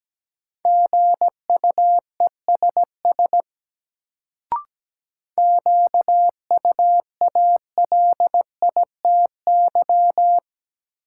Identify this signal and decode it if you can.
Morse code, heard as a keyed tone, about 17 wpm, 700 Hz. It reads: GUESS QUALITY
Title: Sentences from Top 1500 Words - New Word Drill Alt - 17wpm